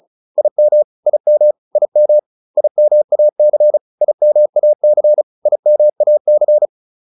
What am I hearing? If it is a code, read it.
Morse code, heard as a keyed tone, about 34 wpm, 600 Hz. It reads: IM IM IM IMAC IMAC IMAC